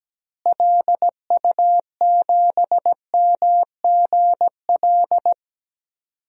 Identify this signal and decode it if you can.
Morse code, heard as a keyed tone, about 17 wpm, 700 Hz. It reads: LU7MGL
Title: DXCC-100 Call Signs - 6 Characters - Repeated 2x - 17wpm